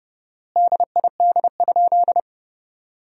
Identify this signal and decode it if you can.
Morse code, heard as a keyed tone, about 30 wpm, 700 Hz. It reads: DID?
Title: 4x Word-Spacing - Sentences from Top 100 Words - 30wpm